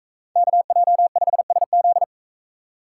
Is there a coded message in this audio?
Yes: Morse code from 0.4 to 2.0 s, heard as a keyed tone, about 42 wpm, 700 Hz. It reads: KJ5SZ